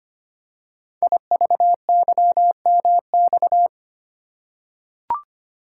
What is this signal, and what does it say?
Morse code, heard as a keyed tone, about 25 wpm, 700 Hz. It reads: IVYMX
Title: Code Groups - 5 Letters - 25wpm